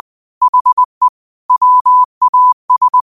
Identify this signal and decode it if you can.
Morse code, heard as a keyed tone, about 20 wpm, 1000 Hz. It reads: HE WAS